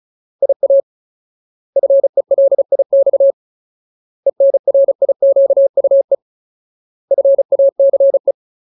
Morse code, heard as a keyed tone, about 36 wpm, 550 Hz. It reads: IA FELIX ENRIQUE FACE
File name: Ultimate Word List / 35WPM Ultimate